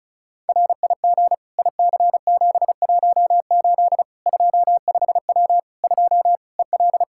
Morse code, heard as a keyed tone, about 36 wpm, 700 Hz. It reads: RIG IC718 25W 2 EL
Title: Practice QSO - Exchange 2 - Info - 35wpm